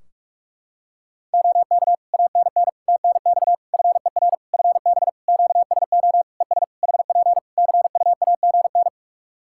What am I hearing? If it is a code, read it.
Morse code, heard as a keyed tone, about 41 wpm, 700 Hz. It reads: OK ANN TNX FER FB QSO ES HP CUAGN
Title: Practice QSO - Exchange 4 - Ending - 45wpm